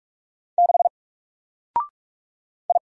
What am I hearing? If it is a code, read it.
Morse code, heard as a keyed tone, about 44 wpm, 700 Hz. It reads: 6 I
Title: Single Letter-Number Alt 45wpm